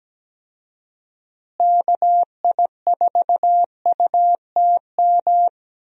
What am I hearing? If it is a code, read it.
Morse code, heard as a keyed tone, about 17 wpm, 700 Hz. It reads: KI4UTM
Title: USA Call Signs - 6 Characters - Repeated 3x - 17wpm